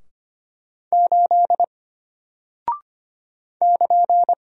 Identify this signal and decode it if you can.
Morse code, heard as a keyed tone, about 25 wpm, 700 Hz. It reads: 8 <KN>
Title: Single Character (Letter:Number:Punctuation:Prosign) Alt 25wpm